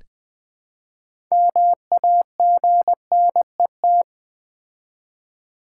Morse code, heard as a keyed tone, about 20 wpm, 700 Hz. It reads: MAGNET
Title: Build A Word - 2 Letters At A Time - 8-Letter Words - 20wpm